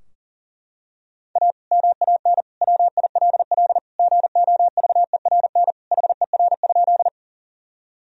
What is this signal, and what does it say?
Morse code, heard as a keyed tone, about 37 wpm, 700 Hz. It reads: A MAN WILL GOVERN HER?
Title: Sentences from Top 1000 Words - Review Part 2 - 40wpm